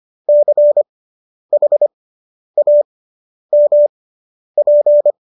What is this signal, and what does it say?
Morse code, heard as a keyed tone, about 25 wpm, spaced as for 10 wpm, 590 Hz. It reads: CHAMP
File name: qso005 10